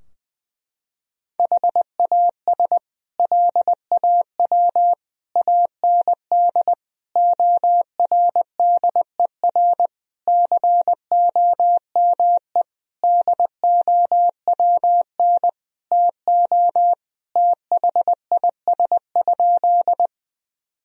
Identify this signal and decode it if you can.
Morse code, heard as a keyed tone, about 20 wpm, 700 Hz. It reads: HAS LAW AND ORDER COME DOWN TO THIS?
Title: Sentences from Top 1000 Words - Review Part 3 - 20wpm